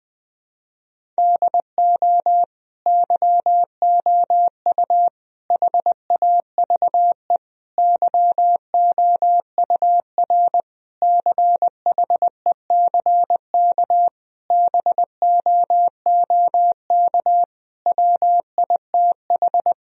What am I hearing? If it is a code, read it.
Morse code, heard as a keyed tone, about 21 wpm, 700 Hz. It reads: DO YOU HAVE YOUR CHECK BOOK WITH
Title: Sentences from Top 1000 Words - Review Part 3 - 20wpm